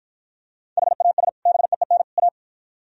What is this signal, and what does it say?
Morse code, heard as a keyed tone, about 51 wpm, 700 Hz. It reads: HAS BEEN A